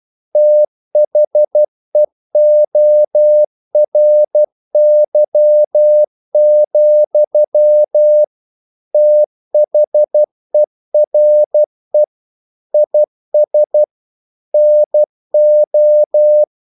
Morse code, heard as a keyed tone, about 12 wpm, 600 Hz. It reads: THEORY, THERE IS NO
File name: CW / Practice / EnglishQuotations0112